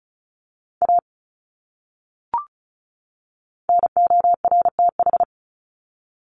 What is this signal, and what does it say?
Morse code, heard as a keyed tone, about 34 wpm, 700 Hz. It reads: A NORTH
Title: Top 300 Words 35wpm